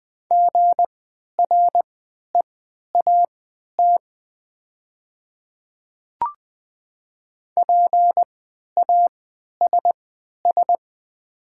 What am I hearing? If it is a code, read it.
Morse code, heard as a keyed tone, about 20 wpm, spaced as for 11 wpm, 700 Hz. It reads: GREAT PASS
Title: 3x Character Spacing - Top 500 Words - 20wpm